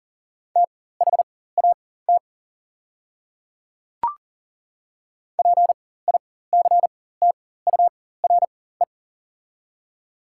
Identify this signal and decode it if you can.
Morse code, heard as a keyed tone, about 40 wpm, spaced as for 19 wpm, 700 Hz. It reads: THAT PICTURE
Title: 4x Character Spacing - Top 200 Words - 40wpm